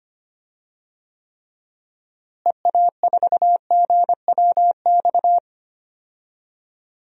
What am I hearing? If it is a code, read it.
Morse code, heard as a keyed tone, about 25 wpm, 700 Hz. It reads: EA4GWX
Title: DXCC-100 Call Signs - 6 Characters - Repeated 2x - 25wpm